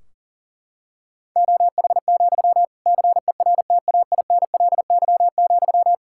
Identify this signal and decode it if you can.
Morse code, heard as a keyed tone, about 37 wpm, 700 Hz. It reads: OH, CERTAINLY,
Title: Sentences from Top 1600 Words 40wpm